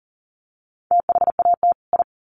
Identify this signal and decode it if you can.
Morse code, heard as a keyed tone, about 40 wpm, 700 Hz. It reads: THAT I
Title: Sets of 2 Words 40wpm